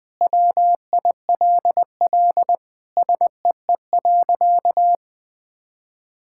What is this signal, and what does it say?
Morse code, heard as a keyed tone, about 21 wpm, 700 Hz. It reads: WILL SEE.
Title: Sentences with 3-Word Phrases - Spoken Beginning - 20wpm